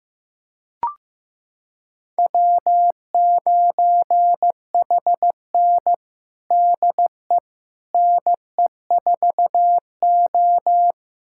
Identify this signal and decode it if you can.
Morse code, heard as a keyed tone, about 16 wpm, 700 Hz. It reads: W9HN DE NE4O
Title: USA Callsign DE Callsign - 4 Characters - 15wpm